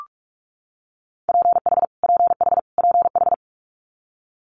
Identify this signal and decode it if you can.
Morse code, heard as a keyed tone, about 46 wpm, 700 Hz. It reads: PH PH PH
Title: Multi-letter Phonemes Repeated 3x 45wpm